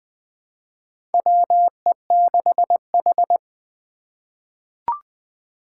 Morse code, heard as a keyed tone, about 20 wpm, 700 Hz. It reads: WE6H
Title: USA Call Signs - 4 Characters - 4x Character Spacing - 20wpm